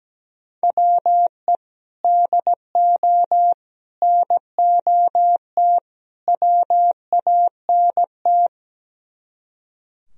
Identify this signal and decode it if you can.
Morse code, heard as a keyed tone, about 17 wpm, 700 Hz. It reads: WE DO NOT WANT
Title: Speed-Racing - Sets of 4 Words 17wpm